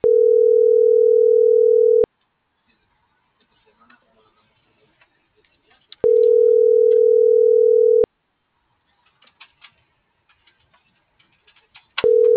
Ringback tone (North America), 440 + 480 Hz.